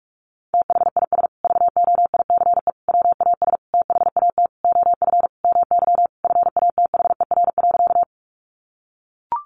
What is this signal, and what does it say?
Morse code, heard as a keyed tone, about 44 wpm, 700 Hz. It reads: THIS VOICE WAS THAT OF MY FATHER.
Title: Sentences from Top 500 Words 45wpm